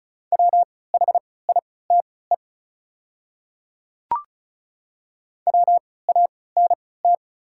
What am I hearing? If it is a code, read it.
Morse code, heard as a keyed tone, about 36 wpm, spaced as for 20 wpm, 700 Hz. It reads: WHITE WANT
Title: 3x Character Spacing - Top 300 Words - 35wpm